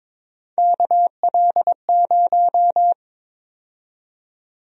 Morse code, heard as a keyed tone, about 21 wpm, 700 Hz. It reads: KL0
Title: USA Call Sign Prefixes - 22wpm